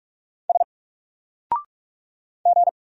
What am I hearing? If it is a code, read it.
Morse code, heard as a keyed tone, about 46 wpm, 700 Hz. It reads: S G
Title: Single Letter-Number Alt 45wpm